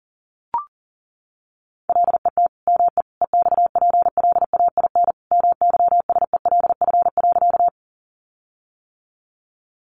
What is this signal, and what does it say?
Morse code, heard as a keyed tone, about 40 wpm, 700 Hz. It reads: LET ME EXPLAIN MYSELF.
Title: Sentences from Top 1300 Words 40wpm